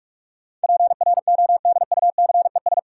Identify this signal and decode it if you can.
Morse code, heard as a keyed tone, about 46 wpm, 700 Hz. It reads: PRODUCES
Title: Sentences from Top 1000 Words - Review Part 6 - New Word Drill - 45wpm